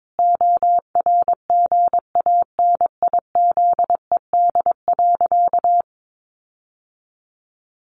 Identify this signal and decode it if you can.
Morse code, heard as a keyed tone, about 22 wpm, 700 Hz. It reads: ORGANIZED.
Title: Sentences from Top 1400 Words 22wpm